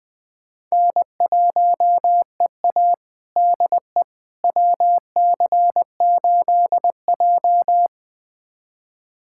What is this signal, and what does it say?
Morse code, heard as a keyed tone, about 20 wpm, 700 Hz. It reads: N1EA DE WC8J
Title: USA Callsign DE Callsign - 4 Characters - 20wpm